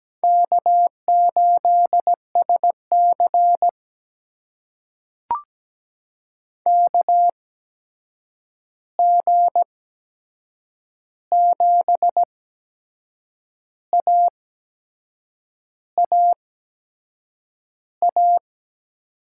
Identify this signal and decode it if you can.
Morse code, heard as a keyed tone, about 17 wpm, spaced as for 6 wpm, 700 Hz. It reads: K8SC KG7AAA
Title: USA Call Signs - 8x Character Spacing - 17wpm